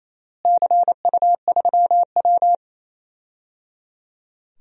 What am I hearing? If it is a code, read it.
Morse code, heard as a keyed tone, about 28 wpm, 700 Hz. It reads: CU3W
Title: DXCC-100 Call Signs - 4 Characters - Repeated 3x - 28wpm